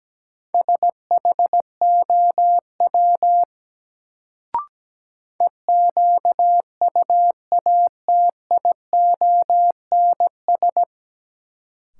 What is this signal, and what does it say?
Morse code, heard as a keyed tone, about 17 wpm, 700 Hz. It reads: SHOW EQUATIONS
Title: Sets of 5 Words - New Word Drill - 17wpm